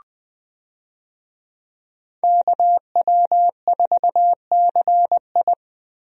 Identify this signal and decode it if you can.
Morse code, heard as a keyed tone, about 20 wpm, 700 Hz. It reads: KW4CI
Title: USA Call Signs - 5 Characters - Repeated 2x - 20wpm